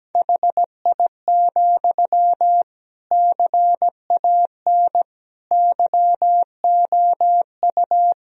Morse code, heard as a keyed tone, about 17 wpm, 700 Hz. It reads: HI, CAN YOU
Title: Sentences from Top 1000 Words - Review Part 3 - 17wpm